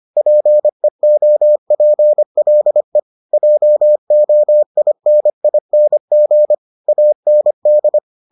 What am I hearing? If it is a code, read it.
Morse code, heard as a keyed tone, about 25 wpm, 600 Hz. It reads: PEOPLE JOINING AND